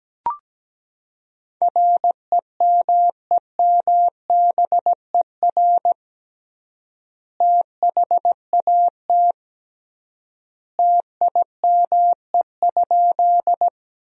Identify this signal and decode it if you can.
Morse code, heard as a keyed tone, about 17 wpm, 700 Hz. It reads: REMEMBER THAT TIME?
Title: 3x Word-Spacing - Sentences from Top 400 Words - 17wpm